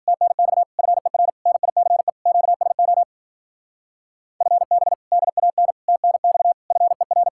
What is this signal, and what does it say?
Morse code, heard as a keyed tone, about 49 wpm, 700 Hz. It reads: TNX FER NICE QSO FB DAN TNX FER
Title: Speed-Racing Common QSO Phrases - 40wpm